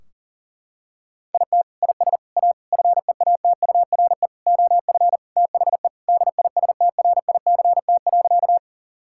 Morse code, heard as a keyed tone, about 40 wpm, 700 Hz. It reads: IT IS A FEATURE OF THE DISTRICT.
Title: Sentences from Top 1600 Words 40wpm